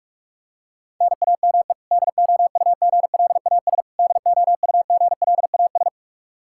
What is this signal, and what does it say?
Morse code, heard as a keyed tone, about 46 wpm, 700 Hz. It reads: NAME DOUGLAS DOUGLAS
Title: Practice QSO - Exchange 1 - Beginning - 45wpm